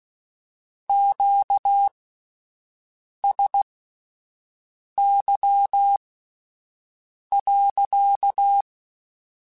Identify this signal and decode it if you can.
Morse code, heard as a keyed tone, about 16 wpm, spaced as for 5 wpm, 800 Hz. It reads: QSY.
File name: Q Calls / Q Calls-mp3s / Q Calls sppc @10